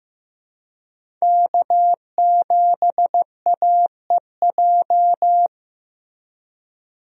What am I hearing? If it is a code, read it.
Morse code, heard as a keyed tone, about 15 wpm, 700 Hz. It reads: K7AEJ
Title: USA Call Signs - 5 Characters - Repeated 2x - 15wpm